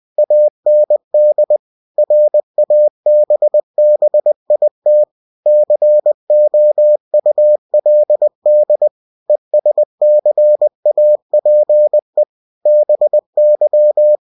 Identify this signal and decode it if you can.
Morse code, heard as a keyed tone, about 20 wpm, 600 Hz. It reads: AND RABBIT COULD ESCAPE BY
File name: cw-books / winnie-the-pooh / winnie-20wpm-page-0091